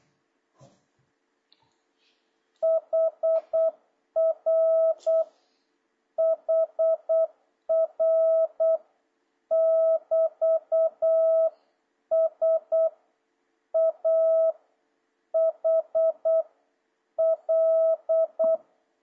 Morse code, heard as a keyed tone, about 8 wpm, 650 Hz. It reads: HR HR <BT> S A H L
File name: jh8cbh / 2026cw